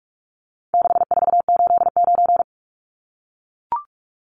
Morse code, heard as a keyed tone, about 44 wpm, 700 Hz. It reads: 6489
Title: Numbers - 4-Digit 45wpm